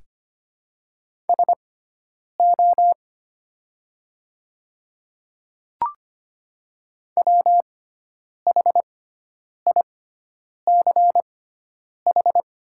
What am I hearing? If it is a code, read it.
Morse code, heard as a keyed tone, about 25 wpm, spaced as for 9 wpm, 700 Hz. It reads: SO WHICH
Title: 6x Character Spacing - Top 100 Words - 25wpm